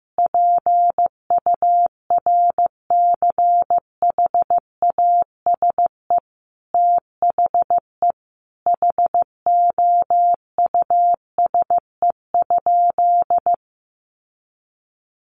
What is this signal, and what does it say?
Morse code, heard as a keyed tone, about 15 wpm, 700 Hz. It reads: PURCHASE THE HOUSE?
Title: Sentences from Top 800 Words 15wpm